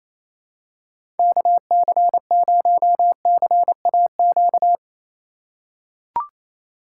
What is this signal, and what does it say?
Morse code, heard as a keyed tone, about 28 wpm, 700 Hz. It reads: KC0CAQ
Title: USA Call Signs - 6 Characters - 2x Character Spacing - 28wpm